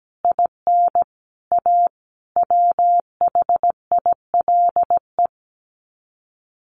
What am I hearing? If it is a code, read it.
Morse code, heard as a keyed tone, about 17 wpm, 700 Hz. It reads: IN A WHILE
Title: Sets of 5 Words 17wpm